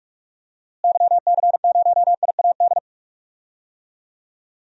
Morse code, heard as a keyed tone, about 47 wpm, 700 Hz. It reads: YC0IAD
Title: DXCC-100 Call Signs - 6 Characters - Repeated 2x - 45wpm